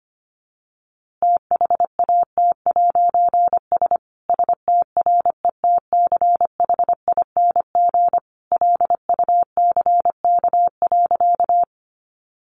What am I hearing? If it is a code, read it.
Morse code, heard as a keyed tone, about 25 wpm, 700 Hz. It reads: THAT'S STRETCHING LUCK.
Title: A Sound Of Thunder 25wpm